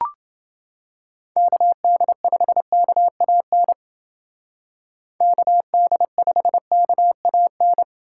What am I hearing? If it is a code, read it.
Morse code, heard as a keyed tone, about 29 wpm, 700 Hz. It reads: KD5KAN KD5KAN